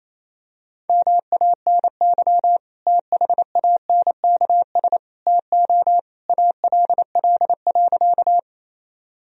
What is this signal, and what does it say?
Morse code, heard as a keyed tone, about 28 wpm, 700 Hz. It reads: MANY THANKS TO ALL.
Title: Sentences from Top 1000 Words - Review Part 3 - 28wpm